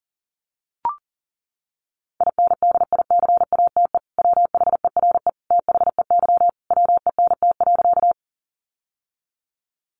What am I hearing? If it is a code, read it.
Morse code, heard as a keyed tone, about 40 wpm, 700 Hz. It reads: INDICATE WHERE THEY WENT.
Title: Sentences from Top 1300 Words 40wpm